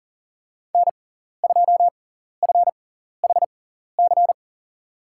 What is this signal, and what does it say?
Morse code, heard as a keyed tone, about 39 wpm, spaced as for 14 wpm, 700 Hz. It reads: N2FHC
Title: USA Call Signs - 5 Characters - 6x Character Spacing - 40wpm